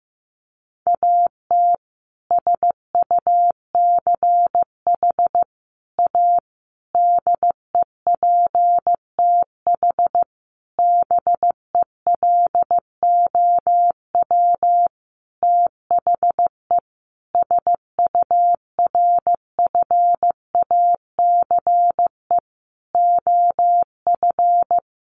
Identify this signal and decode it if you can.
Morse code, heard as a keyed tone, about 15 wpm, 700 Hz. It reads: AT SUCH A DEPTH BELOW THE SURFACE OF
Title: Sentences from Top 1200 Words 15wpm